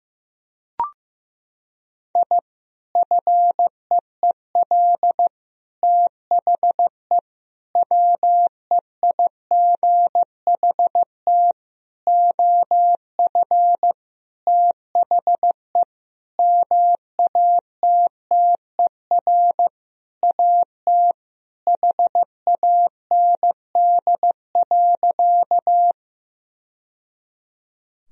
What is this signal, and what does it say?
Morse code, heard as a keyed tone, about 15 wpm, 700 Hz. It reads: I FEEL THE WEIGHT OF THE MATTER AT HAND.
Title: Sentences from Top 1000 Words - Review Part 3 - 15wpm